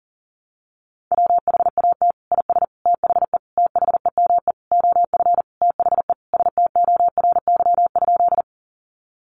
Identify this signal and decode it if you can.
Morse code, heard as a keyed tone, about 40 wpm, 700 Hz. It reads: WHAT IS THE THEME OF THE STORY?
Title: Sentences from Top 1100 Words 40wpm